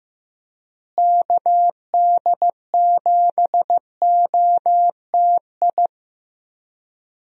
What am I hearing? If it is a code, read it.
Morse code, heard as a keyed tone, about 15 wpm, 700 Hz. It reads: KD7OTI
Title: USA Call Signs - 6 Characters - Repeated 2x - 15wpm